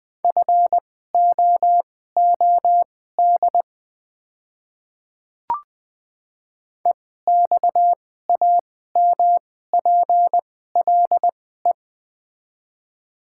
Morse code, heard as a keyed tone, about 20 wpm, spaced as for 14 wpm, 700 Hz. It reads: FOOD EXAMPLE
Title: 2x Character Spacing - Top 300 Words - 20wpm